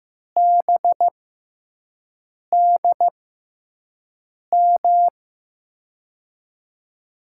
Morse code, heard as a keyed tone, about 15 wpm, spaced as for 5 wpm, 700 Hz. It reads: BDM
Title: USA Call Signs - 6 Characters - 6x Character Spacing - 15wpm